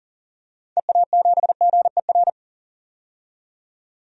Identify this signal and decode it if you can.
Morse code, heard as a keyed tone, about 40 wpm, 700 Hz. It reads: EA7GER
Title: DXCC-100 Call Signs - 6 Characters - 40wpm